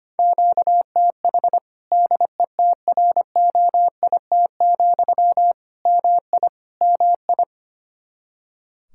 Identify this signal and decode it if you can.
Morse code, heard as a keyed tone, about 25 wpm, 700 Hz. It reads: QTH DETROIT, MI MI
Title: Practice QSO - Exchange 1 - Beginning - 25wpm